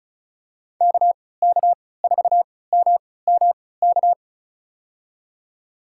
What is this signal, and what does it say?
Morse code, heard as a keyed tone, about 34 wpm, spaced as for 20 wpm, 700 Hz. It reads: KK4MMK
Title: USA Call Signs - 6 Characters - 3x Character Spacing - 35wpm